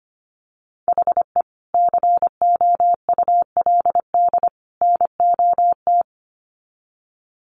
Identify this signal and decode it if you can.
Morse code, heard as a keyed tone, about 25 wpm, 700 Hz. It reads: HE COULD NOT
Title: Sets of 3 Words 25wpm